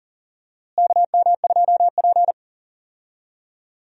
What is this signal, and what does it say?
Morse code, heard as a keyed tone, about 40 wpm, 700 Hz. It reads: KM2P
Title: USA Call Signs - 4 Characters - Repeated 3x - 40wpm